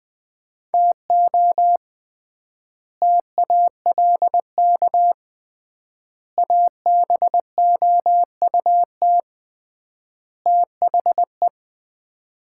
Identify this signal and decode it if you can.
Morse code, heard as a keyed tone, about 20 wpm, 700 Hz. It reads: TO TALK ABOUT THE
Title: 3x Word-Spacing - Sets of 4 Words - 20wpm